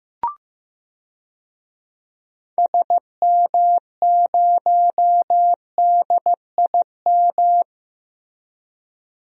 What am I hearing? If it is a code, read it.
Morse code, heard as a keyed tone, about 15 wpm, 700 Hz. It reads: SM0DIM
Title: DXCC-100 Call Signs - 6 Characters - Repeated 2x - 15wpm